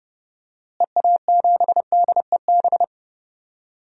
Morse code, heard as a keyed tone, about 32 wpm, 700 Hz. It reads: EA7DEB
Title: DXCC-100 Call Signs - 6 Characters - Repeated 2x - 30wpm